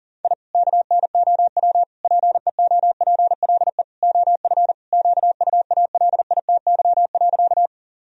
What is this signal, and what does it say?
Morse code, heard as a keyed tone, about 40 wpm, 700 Hz. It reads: I KNOW PEOPLE OF QUALITY.
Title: Sentences from Top 1000 Words - Review Part 4 - 40wpm